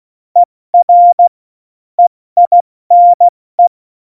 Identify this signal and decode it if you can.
Morse code, heard as a keyed tone, about 16 wpm, spaced as for 14 wpm, 700 Hz. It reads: ER EINE
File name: Morsekurs / Witze / Witz 16. April 2019, 70 F80